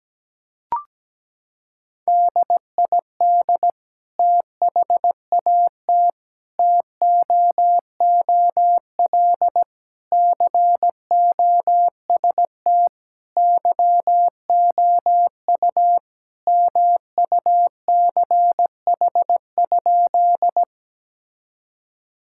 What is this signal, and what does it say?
Morse code, heard as a keyed tone, about 17 wpm, 700 Hz. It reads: DID THAT TOOL COST YOU MUCH?
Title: Sentences from Top 1000 Words - Review Part 6 - 17wpm